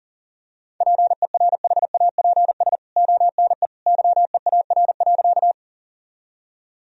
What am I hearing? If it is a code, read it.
Morse code, heard as a keyed tone, about 40 wpm, 700 Hz. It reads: PERHAPS ONE YEAR.
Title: Sentences from Top 1000 Words - Review Part 1 - 40wpm